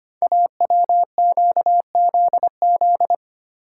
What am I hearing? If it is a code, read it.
Morse code, heard as a keyed tone, about 25 wpm, 700 Hz. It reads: AWQZZ
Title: Code Groups - 5 Letters - 25wpm